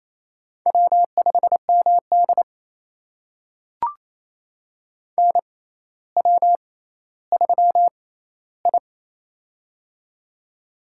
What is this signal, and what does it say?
Morse code, heard as a keyed tone, about 28 wpm, spaced as for 14 wpm, 700 Hz. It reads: W5MD NW3I
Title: USA Call Signs - 4 Characters - 6x Character Spacing - 28wpm